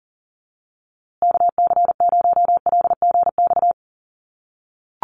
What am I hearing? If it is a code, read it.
Morse code, heard as a keyed tone, about 40 wpm, 700 Hz. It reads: KC0LGX
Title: USA Call Signs Repeated 2x 40wpm